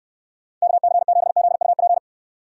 Morse code, heard as a keyed tone, about 69 wpm, 700 Hz. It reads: KC7ZUP